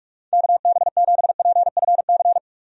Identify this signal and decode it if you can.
Morse code, heard as a keyed tone, about 46 wpm, 700 Hz. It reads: KB7PFC
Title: USA Call Signs - 6 Characters - 45wpm